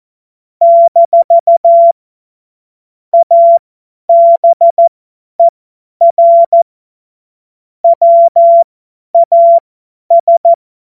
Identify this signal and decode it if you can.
Morse code, heard as a keyed tone, about 14 wpm, spaced as for 10 wpm, 680 Hz. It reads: - ABER WAS
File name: Morsekurs / Witze / Witz 11. Juli 2023 50 F70